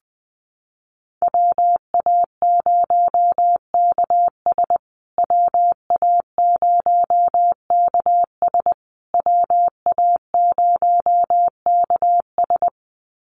Morse code, heard as a keyed tone, about 20 wpm, 700 Hz. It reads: WA0KS WA0KS WA0KS
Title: USA Call Signs Repeated 3x 20wpm